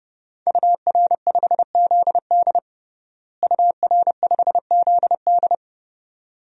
Morse code, heard as a keyed tone, about 32 wpm, 700 Hz. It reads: UR5ZD UR5ZD
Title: DXCC-100 Call Signs - 5 Characters - Repeated 2x - 30wpm